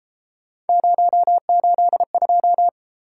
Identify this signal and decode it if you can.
Morse code, heard as a keyed tone, about 32 wpm, 700 Hz. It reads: 082